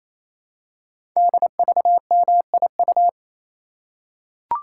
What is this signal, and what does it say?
Morse code, heard as a keyed tone, about 27 wpm, 700 Hz. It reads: DVMIU
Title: Code Groups - 5 Letters - 28wpm